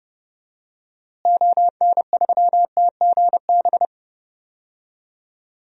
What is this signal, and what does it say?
Morse code, heard as a keyed tone, about 30 wpm, 700 Hz. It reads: ON3TGB
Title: DXCC-100 Call Signs - 6 Characters - Repeated 2x - 30wpm